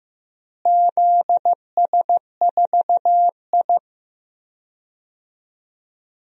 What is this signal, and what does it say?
Morse code, heard as a keyed tone, about 15 wpm, 700 Hz. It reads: ZS4I
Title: DXCC-100 Call Signs - 4 Characters - Repeated 2x - 15wpm